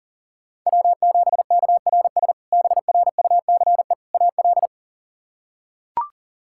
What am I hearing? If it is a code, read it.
Morse code, heard as a keyed tone, about 40 wpm, 700 Hz. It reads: W7KRS BRUCE AL
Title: K1USN SST QSO Practice - 40wpm